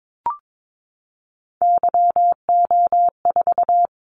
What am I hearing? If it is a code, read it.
Morse code, heard as a keyed tone, about 22 wpm, 700 Hz. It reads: YO4
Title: Callsign Prefixes for Top 25 DXCC - 22wpm